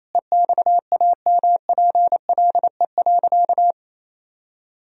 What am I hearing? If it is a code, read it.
Morse code, heard as a keyed tone, about 28 wpm, 700 Hz. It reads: EXAMPLE.
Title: Sentences from Top 1000 Words - Review Part 1 - 28wpm